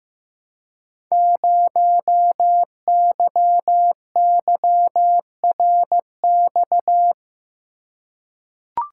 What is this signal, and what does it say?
Morse code, heard as a keyed tone, about 15 wpm, 700 Hz. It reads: 0YYRX
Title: Code Groups - 5 Letter:Numbers - 15wpm